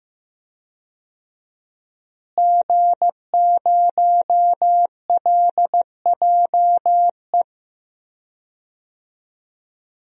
Morse code, heard as a keyed tone, about 15 wpm, 700 Hz. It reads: G0LJE
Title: DXCC-100 Call Signs - 5 Characters - Repeated 2x - 15wpm